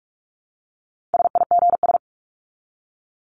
Morse code, heard as a keyed tone, about 44 wpm, 700 Hz. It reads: SIGS
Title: Abbreviations 45wpm